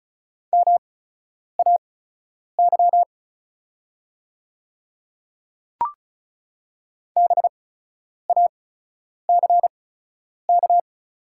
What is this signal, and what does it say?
Morse code, heard as a keyed tone, about 36 wpm, spaced as for 10 wpm, 700 Hz. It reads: MAY BACK
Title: 8x Character Spacing - Top 300 Words - 35wpm